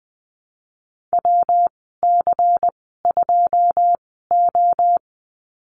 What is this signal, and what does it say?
Morse code, heard as a keyed tone, about 20 wpm, spaced as for 15 wpm, 700 Hz. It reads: WC2O
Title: 161 Lesson 26g - Letter g - Call Signs 20wpm